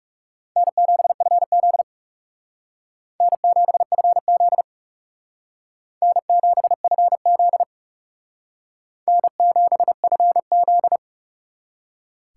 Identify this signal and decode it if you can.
Morse code, heard as a keyed tone, about 36 wpm, 700 Hz. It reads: N7FZ N7FZ N7FZ N7FZ
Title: Speed Racing - USA 1x2 Call Signs 30wpm